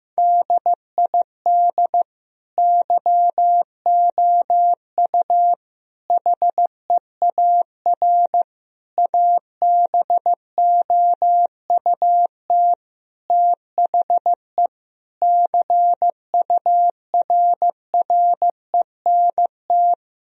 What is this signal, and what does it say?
Morse code, heard as a keyed tone, about 15 wpm, 700 Hz. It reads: DID YOU HEAR ABOUT THE CURRENT
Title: Sentences from Top 1000 Words - Review Part 4 - 15wpm